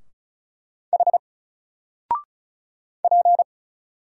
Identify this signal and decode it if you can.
Morse code, heard as a keyed tone, about 36 wpm, 700 Hz. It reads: H P
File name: Single-Letter-Number-35wpm